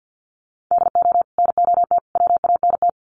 Morse code, heard as a keyed tone, about 50 wpm, 700 Hz. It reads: DO NOT WANT
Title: Sets of 5 Words 50wpm